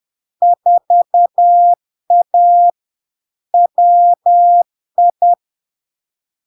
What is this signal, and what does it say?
Morse code, heard as a keyed tone, about 10 wpm, 700 Hz. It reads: 4A WI
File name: LARK-FD-2022-Log-in-CW-10wpm-700-Hz